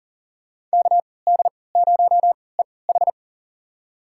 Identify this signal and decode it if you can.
Morse code, heard as a keyed tone, about 40 wpm, spaced as for 23 wpm, 700 Hz. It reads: KD0EH
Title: USA Call Signs - 5 Characters - 3x Character Spacing - 40wpm